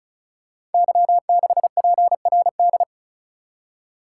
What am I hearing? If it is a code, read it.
Morse code, heard as a keyed tone, about 34 wpm, 690 Hz. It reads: Y6PRD
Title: Code Groups - 5 Letter:Numbers - 35wpm